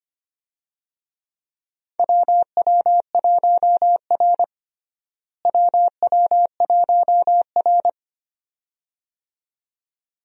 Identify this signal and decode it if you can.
Morse code, heard as a keyed tone, about 25 wpm, 700 Hz. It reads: WW1R WW1R
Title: USA Call Signs - 4 Characters - Repeated 2x - 25wpm